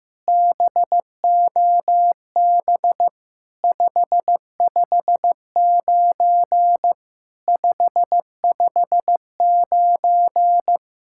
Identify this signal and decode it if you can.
Morse code, heard as a keyed tone, about 15 wpm, 700 Hz. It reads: BOB 559 559
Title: SOTA QSO Practice 15wpm